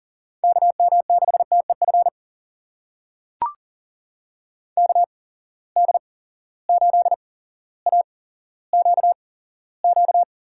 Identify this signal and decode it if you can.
Morse code, heard as a keyed tone, about 40 wpm, spaced as for 17 wpm, 700 Hz. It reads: KM6TEF KD8AQQ
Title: USA Call Signs - 6 Characters - 8x Character Spacing - 40wpm